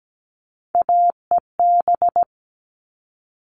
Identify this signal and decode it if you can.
Morse code, heard as a keyed tone, about 17 wpm, 700 Hz. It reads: AEB
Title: All Cut Numbers - 3-Digit 17wpm